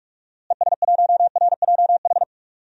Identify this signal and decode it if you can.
Morse code, heard as a keyed tone, about 43 wpm, 700 Hz. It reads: ES1RJH
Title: DXCC-100 Call Signs - 6 Characters - Repeated 2x - 45wpm